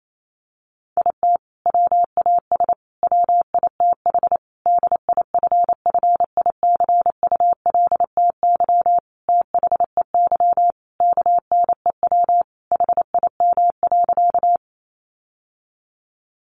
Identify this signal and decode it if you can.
Morse code, heard as a keyed tone, about 28 wpm, 700 Hz. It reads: IT WAS WITH DIFFICULTY THEY KNEW HIM.
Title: Sentences from Top 1500 Words 28wpm